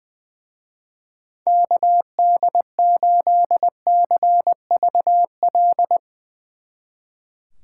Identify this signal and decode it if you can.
Morse code, heard as a keyed tone, about 20 wpm, 700 Hz. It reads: KD8CVL
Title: Speed Racing - USA 2x3 Call Signs 20wpm